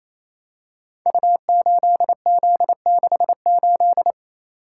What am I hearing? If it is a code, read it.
Morse code, heard as a keyed tone, about 27 wpm, 700 Hz. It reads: U8Z68
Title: Code Groups - 5 Letter:Numbers - 28wpm